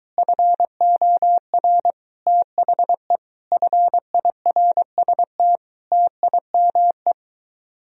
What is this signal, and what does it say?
Morse code, heard as a keyed tone, about 23 wpm, 700 Hz. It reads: FOR THE FIRST TIME